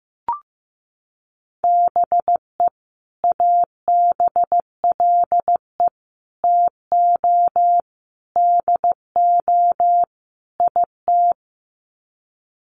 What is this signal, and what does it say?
Morse code, heard as a keyed tone, about 15 wpm, 700 Hz. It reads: BE ABLE TO DO IT
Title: Sets of 5 Words Encore2 15wpm